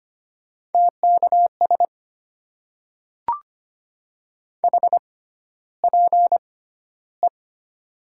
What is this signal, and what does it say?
Morse code, heard as a keyed tone, about 25 wpm, spaced as for 13 wpm, 700 Hz. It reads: TKS HPE
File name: Top 90 QSO Elements - 6x Character Spacing - 25wpm